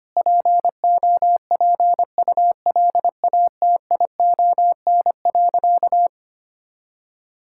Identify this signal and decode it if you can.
Morse code, heard as a keyed tone, about 25 wpm, 700 Hz. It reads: POPULATION.
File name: practice / 25wpm / Sentences with 4-Word Phrases - 25wpm